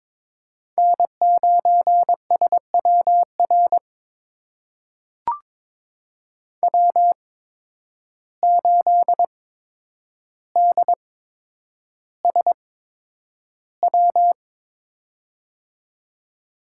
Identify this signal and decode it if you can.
Morse code, heard as a keyed tone, about 22 wpm, spaced as for 9 wpm, 700 Hz. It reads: N9SWR W8DSW
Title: USA Call Signs - 5 Characters - 8x Character Spacing - 22wpm